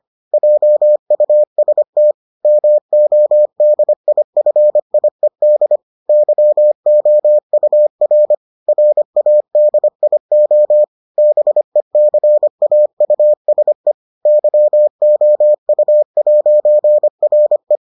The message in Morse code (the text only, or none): JUST MODIFIED YOUR RADIO BECAUSE YOU'RE